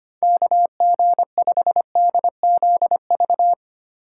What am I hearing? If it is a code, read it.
Morse code, heard as a keyed tone, about 25 wpm, 700 Hz. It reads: KG5DZV
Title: USA Call Signs - 6 Characters - 3x Character Spacing - 25wpm